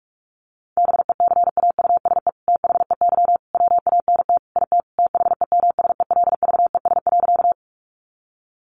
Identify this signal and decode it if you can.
Morse code, heard as a keyed tone, about 44 wpm, 700 Hz. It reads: BECAUSE THEY WANT IT THEMSELVES.
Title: Sentences from Top 800 Words 45wpm